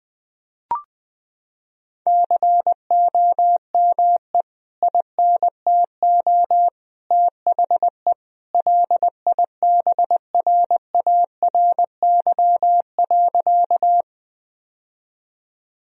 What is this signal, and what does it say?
Morse code, heard as a keyed tone, about 20 wpm, 700 Hz. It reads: COME INTO THE LIBRARY.
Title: Sentences from Top 1000 Words - Review Part 4 - 20wpm